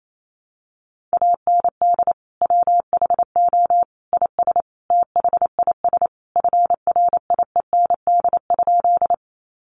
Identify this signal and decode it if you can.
Morse code, heard as a keyed tone, about 28 wpm, 700 Hz. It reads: AND WHO IS THIS FRIEND?
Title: Sentences from Top 300 Words 28wpm